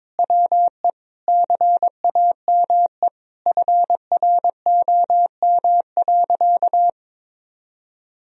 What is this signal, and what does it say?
Morse code, heard as a keyed tone, about 22 wpm, 700 Hz. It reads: WE CAME FROM.
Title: Sentences from Top 1000 Words - Review Part 4 - 22wpm